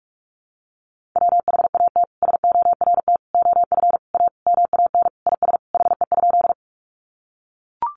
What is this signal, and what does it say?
Morse code, heard as a keyed tone, about 44 wpm, 700 Hz. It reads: WHAT SORT OF A MAN IS HE?
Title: Sentences from Top 1300 Words 45wpm